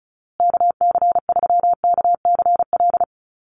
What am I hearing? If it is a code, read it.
Morse code, heard as a keyed tone, about 34 wpm, 700 Hz. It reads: KC3KCL
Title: USA Call Signs Repeated 3x 35wpm